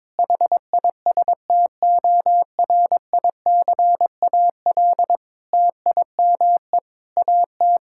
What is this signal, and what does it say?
Morse code, heard as a keyed tone, about 21 wpm, 700 Hz. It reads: HISTORICAL TIME AT
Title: Sentences from Top 1000 Words - Review Part 5 - 22wpm